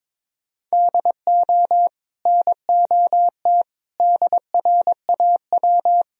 Morse code, heard as a keyed tone, about 22 wpm, 700 Hz. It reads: DO NOT DRAW
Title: Sentences from Top 1000 Words - Review Part 1 - 22wpm